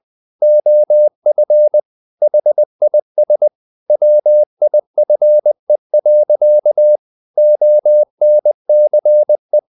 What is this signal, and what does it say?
Morse code, heard as a keyed tone, about 20 wpm, 600 Hz. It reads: OF HIS WIFE. ONCE
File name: the old man and the sea / the old man and the sea part 1 20wpm